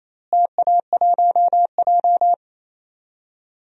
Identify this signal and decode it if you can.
Morse code, heard as a keyed tone, about 28 wpm, 700 Hz. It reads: TA1J
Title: DXCC-100 Call Signs - 4 Characters - Repeated 3x - 28wpm